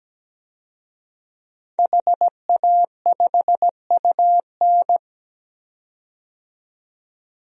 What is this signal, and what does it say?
Morse code, heard as a keyed tone, about 17 wpm, 700 Hz. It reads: HA5UN
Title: DXCC-100 Call Signs - 5 Characters - 17wpm